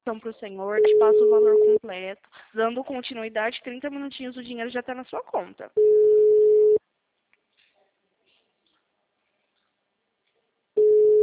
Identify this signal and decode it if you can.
Ringback tone (Europe, 425 Hz), with speech.
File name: out- -2003- - - .129